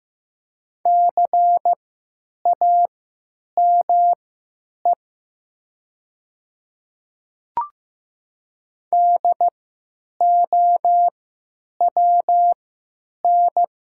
Morse code, heard as a keyed tone, about 15 wpm, spaced as for 9 wpm, 700 Hz. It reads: CAME DOWN
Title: 3x Character Spacing - Top 200 Words - 15wpm